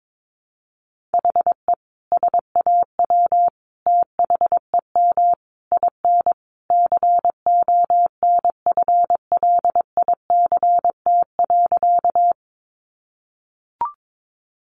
Morse code, heard as a keyed tone, about 22 wpm, 700 Hz. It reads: HE SAW THEM IN CONFLICT.
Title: Sentences from Top 1300 Words 22wpm